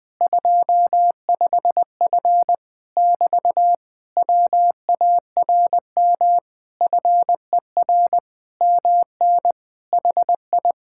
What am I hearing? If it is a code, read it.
Morse code, heard as a keyed tone, about 20 wpm, 700 Hz. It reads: 25F <BT> WARM FER MN HI
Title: Speed-Racing Common QSO Phrases - 17wpm